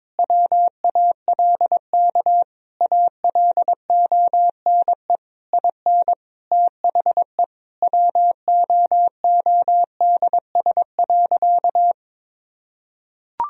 Morse code, heard as a keyed tone, about 21 wpm, 700 Hz. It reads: WALK ALONE IN THE WOODS.
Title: Sentences from Top 1000 Words - Review Part 2 - 22wpm